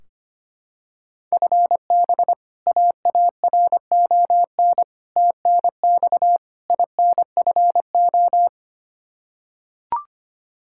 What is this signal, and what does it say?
Morse code, heard as a keyed tone, about 25 wpm, 700 Hz. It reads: FB AARON TNX INFO
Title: K1USN SST QSO Practice 25wpm